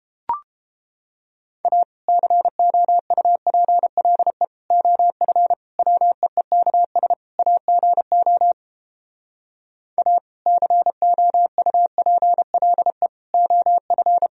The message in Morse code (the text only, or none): A COUPLE OF WEEKS AGO A COUPLE OF